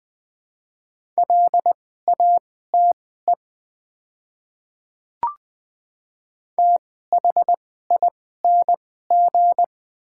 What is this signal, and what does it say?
Morse code, heard as a keyed tone, about 20 wpm, spaced as for 15 wpm, 700 Hz. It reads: LATE THING
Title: 2x Character Spacing - Top 300 Words - 20wpm